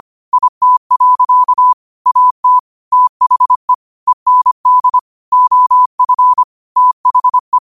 Morse code, heard as a keyed tone, about 25 wpm, 1000 Hz. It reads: IT. AT THE END OF THE
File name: qotdfiles / 25wpm